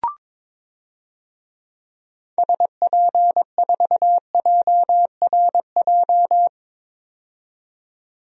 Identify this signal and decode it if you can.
Morse code, heard as a keyed tone, about 22 wpm, 700 Hz. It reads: SP4JRJ
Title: DXCC-100 Call Signs - 6 Characters - 22wpm